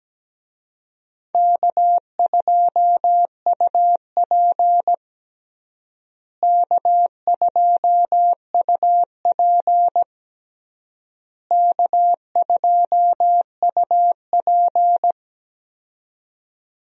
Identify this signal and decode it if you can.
Morse code, heard as a keyed tone, about 17 wpm, 700 Hz. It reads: K2UP K2UP K2UP
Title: USA Call Signs - 4 Characters - Repeated 3x - 17wpm